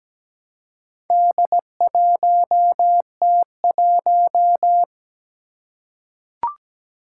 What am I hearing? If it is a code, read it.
Morse code, heard as a keyed tone, about 17 wpm, 700 Hz. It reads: D1T1
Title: Four Challenging Letter-Number Combinations 17wpm